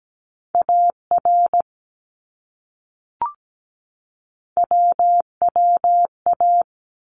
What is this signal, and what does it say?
Morse code, heard as a keyed tone, about 17 wpm, 700 Hz. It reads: AR WWA
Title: ARRL Sections 17wpm